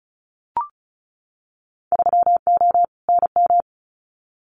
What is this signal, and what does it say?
Morse code, heard as a keyed tone, about 34 wpm, 700 Hz. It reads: 3O NM
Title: Winter Field Day Exchanges Encore 35wpm